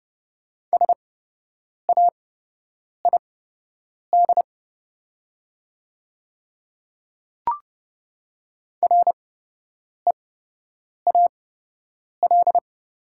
Morse code, heard as a keyed tone, about 30 wpm, spaced as for 8 wpm, 700 Hz. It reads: SAID REAL
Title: 8x Character Spacing - Top 300 Words - 30wpm